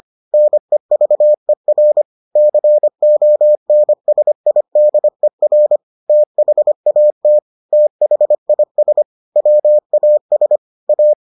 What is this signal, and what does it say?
Morse code, heard as a keyed tone, about 25 wpm, 600 Hz. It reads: NEVER CONSIDER THAT THIS WAS A